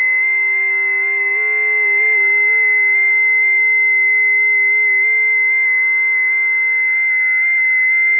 FT8.
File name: tests / 20m busy / test 18